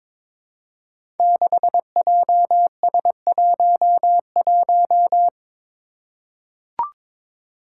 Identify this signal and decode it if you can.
Morse code, heard as a keyed tone, about 22 wpm, 700 Hz. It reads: 6JS11
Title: Five Challenging Letter-Number Combinations 22wpm